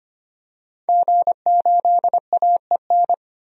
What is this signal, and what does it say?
Morse code, heard as a keyed tone, about 25 wpm, 700 Hz. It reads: G8AEN
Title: DXCC-100 Call Signs - 5 Characters - 25wpm